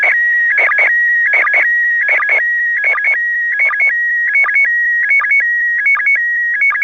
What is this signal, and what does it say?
SSTV (slow-scan television), PD 180.
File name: iss-sstv